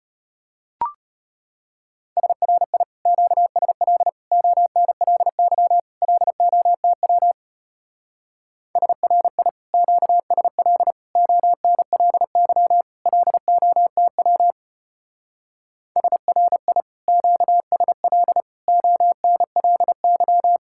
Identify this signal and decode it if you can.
Morse code, heard as a keyed tone, about 31 wpm, 700 Hz. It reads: SRI QSL ONLY LOTW SRI QSL ONLY LOTW SRI QSL ONLY
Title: Speed-Racing Common QSO Phrases - 25wpm